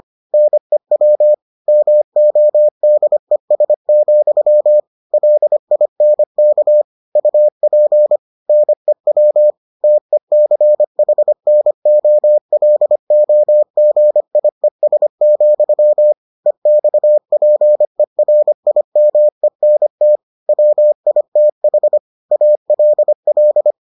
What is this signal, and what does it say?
Morse code, heard as a keyed tone, about 25 wpm, 600 Hz. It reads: NEW MODES, LINK UP NEW TECHNOLOGIES, EXPERIMENT WITH ALL